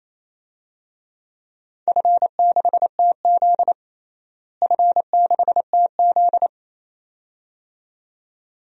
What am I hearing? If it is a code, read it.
Morse code, heard as a keyed tone, about 28 wpm, 700 Hz. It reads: F6TZ F6TZ
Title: DXCC-100 Call Signs - 4 Characters - Repeated 2x - 28wpm